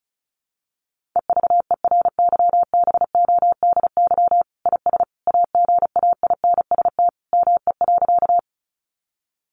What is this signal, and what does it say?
Morse code, heard as a keyed tone, about 34 wpm, 700 Hz. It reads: EVERYBODY IS AGAINST ME.
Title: Sentences from Top 1400 Words 35wpm